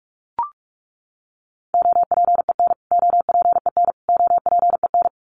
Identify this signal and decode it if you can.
Morse code, heard as a keyed tone, about 44 wpm, 700 Hz. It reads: OPEN OPEN OPEN
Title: Top 500 Words Repeated 45wpm